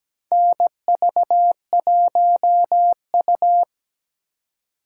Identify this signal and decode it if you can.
Morse code, heard as a keyed tone, about 17 wpm, 700 Hz. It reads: NV1U
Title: K1USN SST QSO Practice - 17wpm